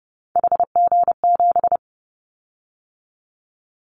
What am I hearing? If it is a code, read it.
Morse code, heard as a keyed tone, about 30 wpm, 700 Hz. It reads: HG7
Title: Callsign Prefixes for Top 50 DXCC - 30wpm